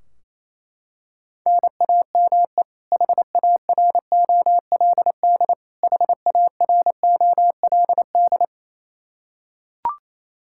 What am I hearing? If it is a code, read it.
Morse code, heard as a keyed tone, about 27 wpm, 700 Hz. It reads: NAME HAROLD HAROLD
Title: K1USN SST QSO Practice 28wpm